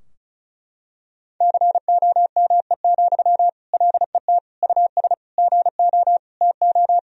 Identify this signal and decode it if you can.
Morse code, heard as a keyed tone, about 36 wpm, 700 Hz. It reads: COME, LET US GO TO
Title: Sentences from Top 1000 Words - Review Part 4 - 35wpm